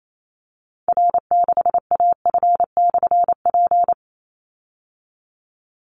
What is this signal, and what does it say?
Morse code, heard as a keyed tone, about 28 wpm, 700 Hz. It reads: R6AF/P
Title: Call Signs Repeated 2x - 28wpm